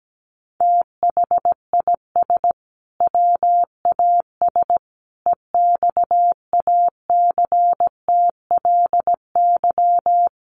Morse code, heard as a keyed tone, about 17 wpm, 700 Hz. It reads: THIS WAS EXACTLY
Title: Sentences from Top 900 Words 17wpm